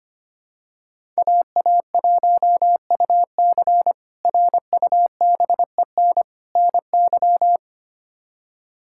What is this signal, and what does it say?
Morse code, heard as a keyed tone, about 25 wpm, 700 Hz. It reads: AA1UC RUBEN NY
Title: K1USN SST QSO Practice - 25wpm